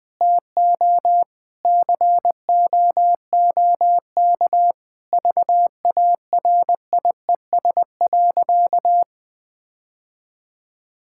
Morse code, heard as a keyed tone, about 20 wpm, 700 Hz. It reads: TO COOK VARIES.
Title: Sentences with 7-Word Phrases - 20wpm